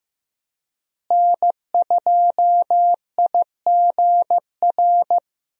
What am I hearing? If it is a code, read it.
Morse code, heard as a keyed tone, about 15 wpm, 700 Hz. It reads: N2IGR
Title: USA Call Signs - 5 Characters - Repeated 2x - 15wpm